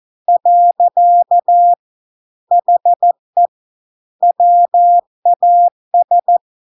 Morse code, heard as a keyed tone, about 14 wpm, 700 Hz. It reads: . HE WAS